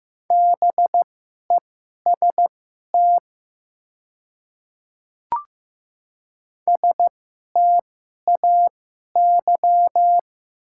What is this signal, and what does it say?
Morse code, heard as a keyed tone, about 15 wpm, spaced as for 11 wpm, 700 Hz. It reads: BEST STAY
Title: 2x Character Spacing - Top 500 Words - 15wpm